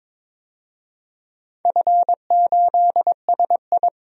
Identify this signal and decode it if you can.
Morse code, heard as a keyed tone, about 22 wpm, 700 Hz. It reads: F8SI
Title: DXCC-100 Call Signs - 4 Characters - 22wpm